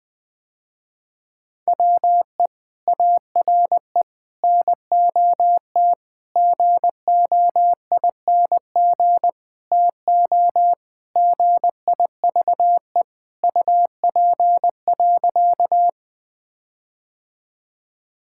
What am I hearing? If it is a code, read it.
Morse code, heard as a keyed tone, about 20 wpm, 700 Hz. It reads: WE ARE NOT GOING TO GIVE UP.
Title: Sentences with 5-Word Phrases - Spoken Beginning - 20wpm